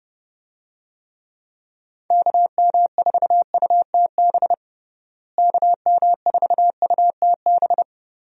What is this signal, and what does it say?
Morse code, heard as a keyed tone, about 30 wpm, 700 Hz. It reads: KM4UTB KM4UTB
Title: USA Call Signs - 6 Characters - Repeated 2x - 30wpm